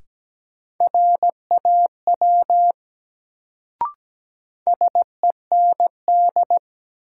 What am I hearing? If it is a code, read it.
Morse code, heard as a keyed tone, about 17 wpm, 700 Hz. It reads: RAW SEND
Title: Sentences from Top 1500 Words - New Word Drill - 17wpm